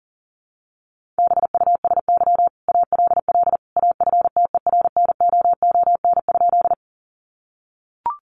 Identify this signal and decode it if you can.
Morse code, heard as a keyed tone, about 40 wpm, 700 Hz. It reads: BUSY ALL AFTERNOON?
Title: Sentences from Top 1500 Words 40wpm